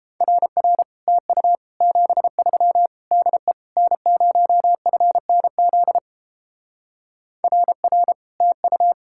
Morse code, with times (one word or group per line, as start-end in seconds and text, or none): 0.2-0.8 RR
1.1-1.6 TU
1.8-2.9 73
3.1-3.5 DE
3.8-6.0 N0FNZ
7.4-8.1 RR
8.4-8.9 TU